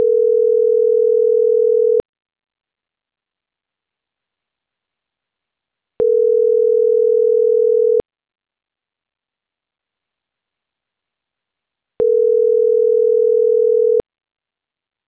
Ringback tone (North America), 440 + 480 Hz.